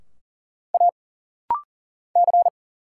Morse code, heard as a keyed tone, about 40 wpm, 700 Hz. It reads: A C
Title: Single Letters - Rapid-Fire 40wpm